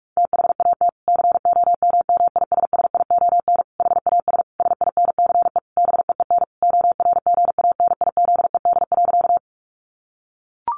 Morse code, heard as a keyed tone, about 44 wpm, 700 Hz. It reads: THAT COMMISSION HAS SINCE BEEN ORGANIZED.
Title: Sentences from Top 1400 Words 45wpm